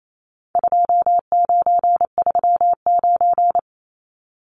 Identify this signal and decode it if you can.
Morse code, heard as a keyed tone, about 28 wpm, 700 Hz. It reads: 2939
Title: Numbers - 4-Digit 28wpm